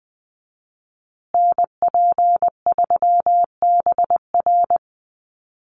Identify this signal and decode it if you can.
Morse code, heard as a keyed tone, about 20 wpm, 700 Hz. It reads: NP3BR
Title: USA Call Signs Repeated 2x 20wpm